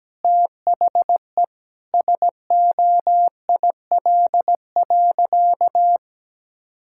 Morse code, heard as a keyed tone, about 17 wpm, 700 Hz. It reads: THE SOIL.
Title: Sentences from Top 1600 Words 17wpm